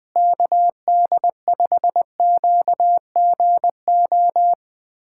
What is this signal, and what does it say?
Morse code, heard as a keyed tone, about 21 wpm, 700 Hz. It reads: KD5QGO
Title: USA Callsign DE Callsign 20wpm